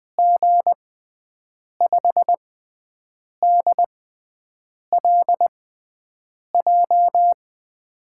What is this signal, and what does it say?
Morse code, heard as a keyed tone, about 20 wpm, spaced as for 7 wpm, 700 Hz. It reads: G5DLJ
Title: USA Call Signs - 6 Characters - 6x Character Spacing - 20wpm